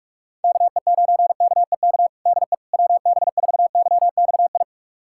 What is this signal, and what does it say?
Morse code, heard as a keyed tone, about 46 wpm, 690 Hz. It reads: KE9KEK DE WB4YXI
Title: USA Callsign DE Callsign - 6 Characters - 45wpm